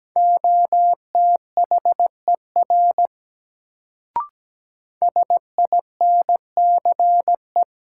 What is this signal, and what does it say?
Morse code, heard as a keyed tone, about 17 wpm, 700 Hz. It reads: OTHER SINCE
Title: Sets of 2 Words - New Word Drill - 17wpm